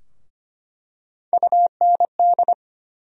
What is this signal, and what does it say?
Morse code, heard as a keyed tone, about 25 wpm, 700 Hz. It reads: UND
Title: USA Call Sign Postfixes - 3 Letters 25wpm